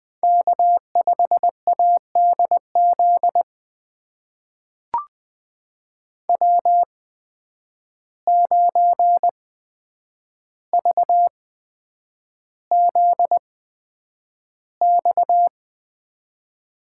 Morse code, heard as a keyed tone, about 20 wpm, spaced as for 9 wpm, 700 Hz. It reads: K5ADZ W9VZX
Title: USA Call Signs - 5 Characters - 8x Character Spacing - 20wpm